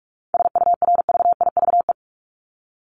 Morse code, heard as a keyed tone, about 44 wpm, 700 Hz. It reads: SURVIVE
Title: Top 1251-1500 Words 45wpm